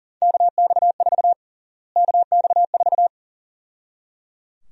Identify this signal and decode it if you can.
Morse code, heard as a keyed tone, about 37 wpm, 700 Hz. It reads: KX4 KX4
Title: USA Call Sign Prefixes - Repeated 2x - 40wpm